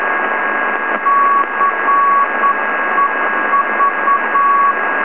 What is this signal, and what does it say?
Morse code, heard as a keyed tone, about 9 wpm, 1120 Hz, with radio noise. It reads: CEV
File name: LongWaves / Beacon / ferrite